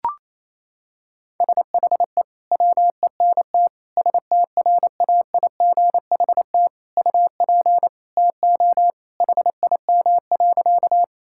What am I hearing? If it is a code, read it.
Morse code, heard as a keyed tone, about 28 wpm, 700 Hz. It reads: SHE WENT STRAIGHT UP TO HIM.
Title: Sentences from Top 1000 Words - Review Part 6 - 28wpm